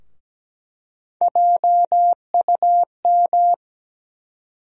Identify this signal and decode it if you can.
Morse code, heard as a keyed tone, about 17 wpm, 700 Hz. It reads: JUM
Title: USA Call Sign Postfixes - 3 Letters 17wpm